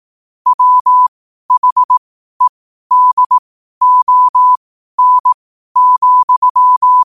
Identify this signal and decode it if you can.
Morse code, heard as a keyed tone, about 18 wpm, spaced as for 12 wpm, 1000 Hz. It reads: WHEDON,
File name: qotdfiles / 12wpm